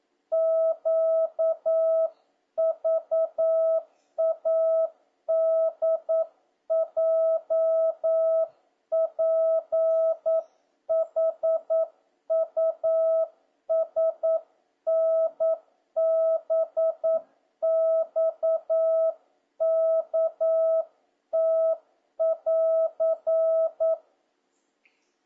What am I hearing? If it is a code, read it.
Morse code, heard as a keyed tone, about 9 wpm, 650 Hz. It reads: QVADJPHUSNBXKT<AR>